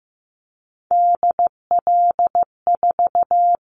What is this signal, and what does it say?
Morse code, heard as a keyed tone, about 15 wpm, 700 Hz. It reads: DL4
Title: Callsign Prefixes for Top 25 DXCC - 15wpm